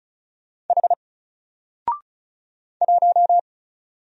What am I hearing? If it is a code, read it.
Morse code, heard as a keyed tone, about 33 wpm, 700 Hz. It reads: H 1
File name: Single-Letter-Number-35wpm